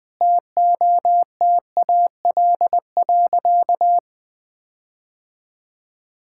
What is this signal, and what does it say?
Morse code, heard as a keyed tone, about 20 wpm, 700 Hz. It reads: TOTAL.
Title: Sentences with 8-Word Phrases - 20wpm